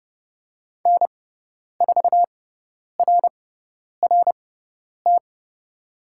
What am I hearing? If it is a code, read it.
Morse code, heard as a keyed tone, about 30 wpm, spaced as for 10 wpm, 700 Hz. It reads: N4RRT
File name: Lesson 16g - Letter c - Call Signs 30wpm